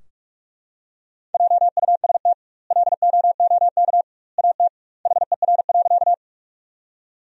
Morse code, heard as a keyed tone, about 41 wpm, 700 Hz. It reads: JUST LOOK AT HER.
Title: 2x Word-Spacing - Sentences from Top 200 Words - 45wpm